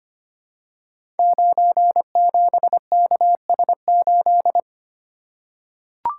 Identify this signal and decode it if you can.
Morse code, heard as a keyed tone, about 25 wpm, 700 Hz. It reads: 97KS8
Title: Code Groups - 5 Letter:Numbers - 25wpm